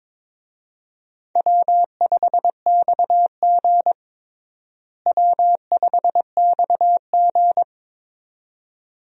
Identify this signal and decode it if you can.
Morse code, heard as a keyed tone, about 21 wpm, 700 Hz. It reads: W5XG W5XG
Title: USA Call Signs - 4 Characters - Repeated 2x - 22wpm